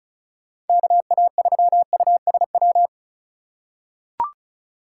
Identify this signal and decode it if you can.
Morse code, heard as a keyed tone, about 34 wpm, 700 Hz. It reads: KA3USW
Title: USA Call Signs - 6 Characters - 2x Character Spacing - 35wpm